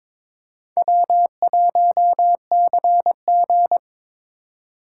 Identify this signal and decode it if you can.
Morse code, heard as a keyed tone, about 26 wpm, 700 Hz. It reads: W1CG
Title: Speed Racing - USA Call Signs 22wpm